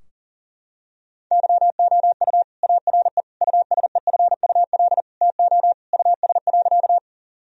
Morse code, heard as a keyed tone, about 40 wpm, 700 Hz. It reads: YOU ARE USEFUL TO US.
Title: Sentences from Top 1000 Words - Review Part 4 - 40wpm